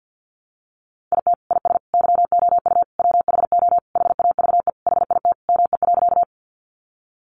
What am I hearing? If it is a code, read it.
Morse code, heard as a keyed tone, about 50 wpm, 700 Hz. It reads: IT IS YOU WHO HAVE HIT ME.
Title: Sentences from Top 1200 Words 50wpm